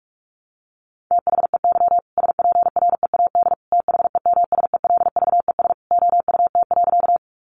Morse code, heard as a keyed tone, about 44 wpm, 700 Hz. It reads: THEY SPREAD THEMSELVES OUT.
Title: Sentences from Top 1200 Words 45wpm